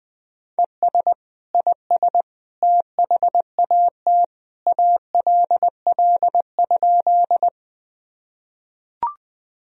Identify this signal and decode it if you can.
Morse code, heard as a keyed tone, about 21 wpm, 700 Hz. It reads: ES IS THAT ALL?
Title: Sentences from Top 100 Words and Abbreviations 20wpm